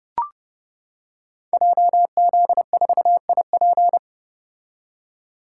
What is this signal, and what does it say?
Morse code, heard as a keyed tone, about 30 wpm, 700 Hz. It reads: JZ4IP
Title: Code Groups - 5 Letter:Numbers - 30wpm